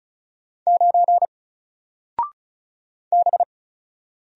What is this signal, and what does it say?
Morse code, heard as a keyed tone, about 36 wpm, 700 Hz. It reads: 9 B
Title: Single Letter-Number Alt 35wpm